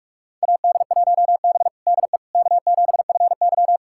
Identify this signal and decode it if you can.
Morse code, heard as a keyed tone, about 42 wpm, 700 Hz. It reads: AD1B DE K7FY
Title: USA Callsign DE Callsign - 4 Characters - 45wpm